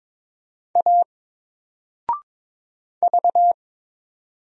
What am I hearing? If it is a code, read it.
Morse code, heard as a keyed tone, about 22 wpm, 700 Hz. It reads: A V
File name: Single-Letter-Number-22wpm